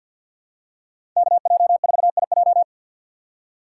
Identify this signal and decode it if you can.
Morse code, heard as a keyed tone, about 50 wpm, 700 Hz. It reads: KJ4IJ
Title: USA Call Signs - 5 Characters - 2x Character Spacing - 50wpm